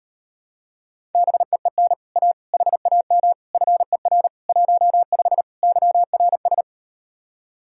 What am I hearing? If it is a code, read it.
Morse code, heard as a keyed tone, about 38 wpm, 700 Hz. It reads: BEEN A HAM FER 15 YRS